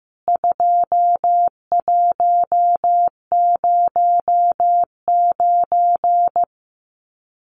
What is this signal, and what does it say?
Morse code, heard as a keyed tone, about 15 wpm, 700 Hz. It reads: 2109
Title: CWT Exchange 15wpm